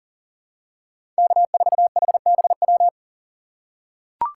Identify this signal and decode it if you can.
Morse code, heard as a keyed tone, about 40 wpm, 700 Hz. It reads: K4HBW
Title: USA Call Signs - 5 Characters - 4x Character Spacing - 40wpm